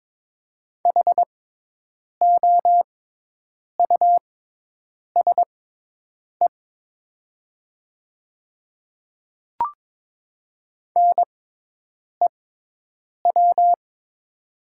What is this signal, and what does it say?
Morse code, heard as a keyed tone, about 22 wpm, spaced as for 8 wpm, 700 Hz. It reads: HOUSE NEW
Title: 6x Character Spacing - Top 200 Words - 22wpm